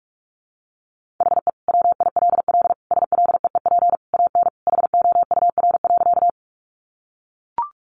Morse code, heard as a keyed tone, about 44 wpm, 700 Hz. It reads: HE WILL SLEEP AN HOUR.
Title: Sentences from Top 500 Words 45wpm